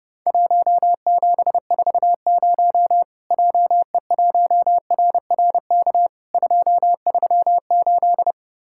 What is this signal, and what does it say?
Morse code, heard as a keyed tone, about 30 wpm, 700 Hz. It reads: 1740 JE1RRK 238
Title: Worked All Europe DX Contest - QTC Exchange - 30wpm